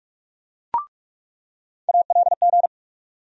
Morse code, heard as a keyed tone, about 44 wpm, 700 Hz. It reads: ALG
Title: USA Call Sign Postfixes - 3 Letters 45wpm